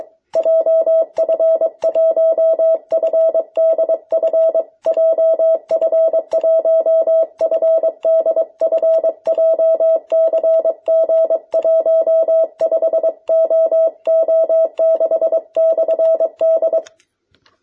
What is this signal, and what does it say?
Morse code, heard as a keyed tone, about 23 wpm, 650 Hz. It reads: JF1FDF JF1FDF JCG15OO6/D
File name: jh8cbh / 2021cw